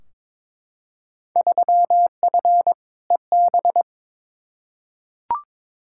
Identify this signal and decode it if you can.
Morse code, heard as a keyed tone, about 22 wpm, 700 Hz. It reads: 3F EB
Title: Field Day Exchanges - 22wpm